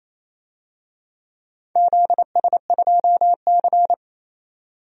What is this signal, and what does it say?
Morse code, heard as a keyed tone, about 27 wpm, 700 Hz. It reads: ZS2C
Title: DXCC-100 Call Signs - 4 Characters - 28wpm